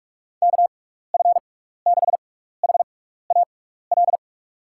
Morse code, heard as a keyed tone, about 46 wpm, spaced as for 15 wpm, 700 Hz. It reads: KF6HAL
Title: USA Call Signs - 6x Character Spacing - 45wpm